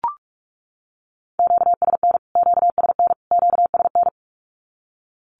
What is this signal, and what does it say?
Morse code, heard as a keyed tone, about 44 wpm, 700 Hz. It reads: QSN QSN QSN
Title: Q-Codes Repeated 3x 45wpm